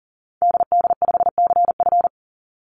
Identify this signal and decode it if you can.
Morse code, heard as a keyed tone, about 38 wpm, 700 Hz. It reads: DD5CF
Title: Call Signs - 40wpm